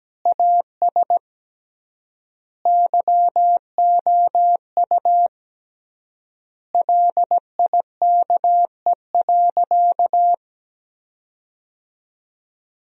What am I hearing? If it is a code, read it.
Morse code, heard as a keyed tone, about 17 wpm, 700 Hz. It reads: AS YOU LIKE.
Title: 3x Word-Spacing - Sentences from Top 500 Words - 17wpm